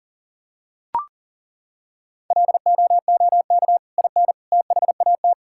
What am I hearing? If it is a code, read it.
Morse code, heard as a keyed tone, about 37 wpm, 700 Hz. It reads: LOOK IN THAT
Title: Sentences from Top 1000 Words - Review Part 5 - 40wpm